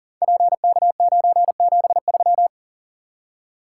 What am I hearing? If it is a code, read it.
Morse code, heard as a keyed tone, about 39 wpm, 690 Hz. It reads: PK973
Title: Code Groups - 5 Letter:Numbers - 40wpm